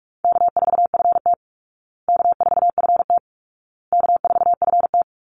Morse code, heard as a keyed tone, about 44 wpm, 700 Hz. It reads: K4FT K4FT K4FT
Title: Easy Call Signs Repeated 3x - 45wpm